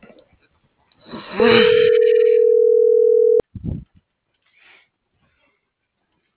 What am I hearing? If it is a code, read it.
Ringback tone (North America), 440 + 480 Hz.